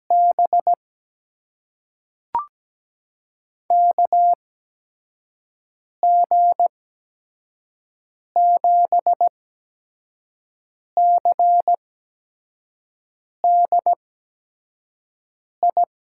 Morse code, heard as a keyed tone, about 17 wpm, spaced as for 5 wpm, 700 Hz. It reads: B KG7CDI
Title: USA Call Signs - 6 Characters - 8x Character Spacing - 17wpm